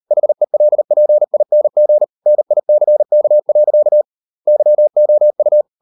Morse code, heard as a keyed tone, about 40 wpm, 600 Hz. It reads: HELPING NICK. YOU